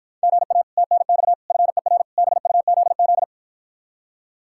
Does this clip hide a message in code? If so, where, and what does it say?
Yes: Morse code from 0.2 to 3.2 s, heard as a keyed tone, about 53 wpm, 700 Hz. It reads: GA TNX FER BUZZ